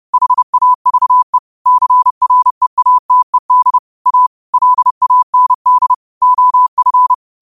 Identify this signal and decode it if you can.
Morse code, heard as a keyed tone, about 30 wpm, 1000 Hz. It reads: HAVE CREATED A LAND OF
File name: qotdfiles / 30wpm